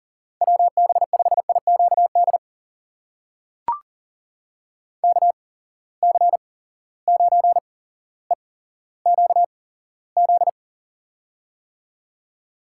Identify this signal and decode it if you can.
Morse code, heard as a keyed tone, about 42 wpm, spaced as for 17 wpm, 700 Hz. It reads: WB5IQD KC9EQZ
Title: USA Call Signs - 6 Characters - 8x Character Spacing - 40wpm